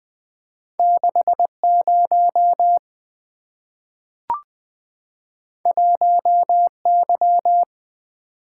Morse code, heard as a keyed tone, about 20 wpm, 700 Hz. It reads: 6 0 1 Y
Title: Two Letter-Number Combinations - 20wpm